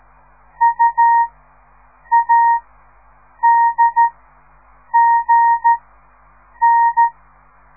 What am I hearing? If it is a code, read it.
Morse code, heard as a keyed tone, about 13 wpm, spaced as for 7 wpm, 930 Hz. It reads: UADGN
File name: radiomorningstar / morse / les 4